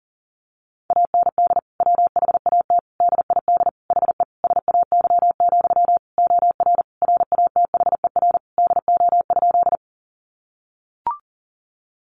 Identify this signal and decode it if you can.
Morse code, heard as a keyed tone, about 40 wpm, 700 Hz. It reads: AND WHAT DID HE SAY, OR RATHER DO?
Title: Sentences from Top 700 Words 40wpm